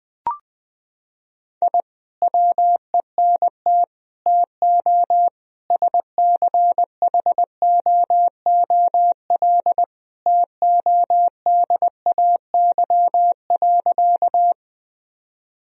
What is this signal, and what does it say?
Morse code, heard as a keyed tone, about 20 wpm, 700 Hz. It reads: I WENT TO SCHOOL TODAY.
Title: Sentences from Top 1000 Words - Review Part 1 - 20wpm